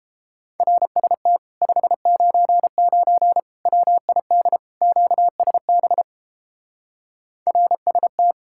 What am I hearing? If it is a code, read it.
Morse code, heard as a keyed tone, about 31 wpm, 700 Hz. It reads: RST 599 WID QSB RST